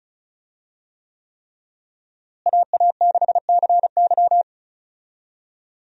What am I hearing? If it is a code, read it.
Morse code, heard as a keyed tone, about 36 wpm, 700 Hz. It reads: AA6CY
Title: USA Call Signs - 5 Characters - Repeated 3x - 35wpm